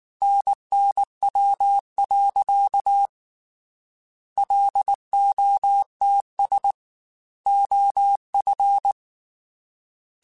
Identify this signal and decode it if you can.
Morse code, heard as a keyed tone, about 19 wpm, 800 Hz. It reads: NNW. LOTS OF
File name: Weather Sets / Weather Sets-mp3s / Weather 5 12 @18